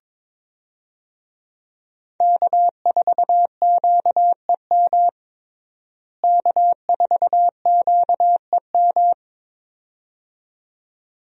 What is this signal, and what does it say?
Morse code, heard as a keyed tone, about 22 wpm, 700 Hz. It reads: K4QEM K4QEM
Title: USA Call Signs - 5 Characters - Repeated 2x - 22wpm